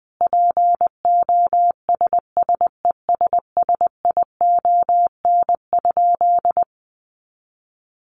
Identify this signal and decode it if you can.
Morse code, heard as a keyed tone, about 20 wpm, 700 Hz. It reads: POSSESSION?
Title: Sentences from Top 1400 Words 20wpm